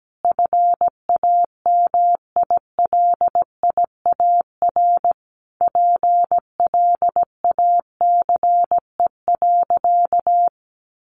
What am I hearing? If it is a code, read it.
Morse code, heard as a keyed tone, about 17 wpm, 700 Hz. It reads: FAMILIAR PLACE.
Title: Sentences from Top 1500 Words 17wpm